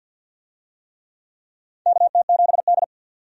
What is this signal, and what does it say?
Morse code, heard as a keyed tone, about 47 wpm, 700 Hz. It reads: KT7D
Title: USA Call Signs - 4 Characters - 50wpm